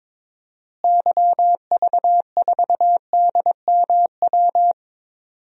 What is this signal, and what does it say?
Morse code, heard as a keyed tone, about 22 wpm, 700 Hz. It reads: YV4DMW
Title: DXCC-100 Call Signs - 6 Characters - Repeated 2x - 22wpm